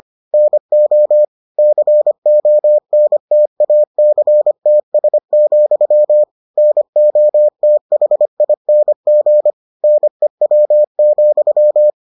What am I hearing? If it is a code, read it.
Morse code, heard as a keyed tone, about 25 wpm, 600 Hz. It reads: NO CONTACTS, NOTHING NEW,